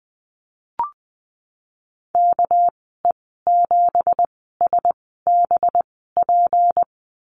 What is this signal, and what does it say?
Morse code, heard as a keyed tone, about 20 wpm, spaced as for 15 wpm, 700 Hz. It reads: KE7SBP
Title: 207 Lesson 33g - Letter j - Call Signs 20wpm